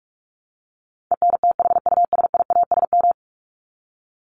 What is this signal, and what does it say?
Morse code, heard as a keyed tone, about 44 wpm, 700 Hz. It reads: ENTHUSIASM
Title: Top 1751-2000 Words 45wpm